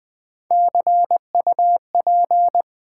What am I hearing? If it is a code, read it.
Morse code, heard as a keyed tone, about 20 wpm, 700 Hz. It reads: CUP
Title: Easy Words with Hints - 20wpm